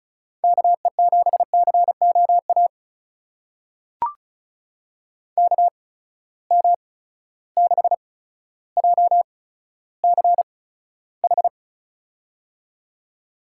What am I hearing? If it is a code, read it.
Morse code, heard as a keyed tone, about 36 wpm, spaced as for 15 wpm, 700 Hz. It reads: KE7COA KM6JCH
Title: USA Call Signs - 6 Characters - 8x Character Spacing - 35wpm